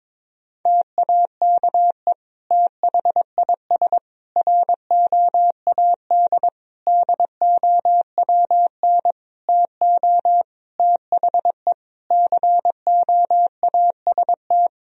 Morse code, heard as a keyed tone, about 21 wpm, 700 Hz. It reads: TAKE THIS ROAD DOWN TO THE COAST
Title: Sentences from Top 1000 Words - Review Part 4 - 22wpm